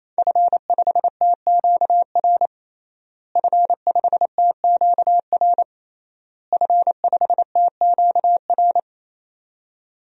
Morse code, heard as a keyed tone, about 28 wpm, 700 Hz. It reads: F5TQR F5TQR F5TQR
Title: DXCC-100 Call Signs - 5 Characters - Repeated 3x - 28wpm